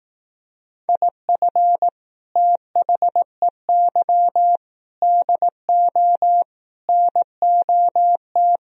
Morse code, heard as a keyed tone, about 18 wpm, 700 Hz. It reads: IF THEY DO NOT
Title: Speed-Racing - Sets of 4 Words 15wpm